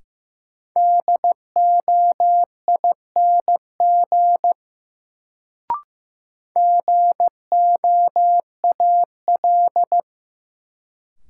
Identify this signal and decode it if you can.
Morse code, heard as a keyed tone, about 15 wpm, 700 Hz. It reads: DOING GOAL
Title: Sentences from Top 1400 Words - New Word Drill Alt - 15wpm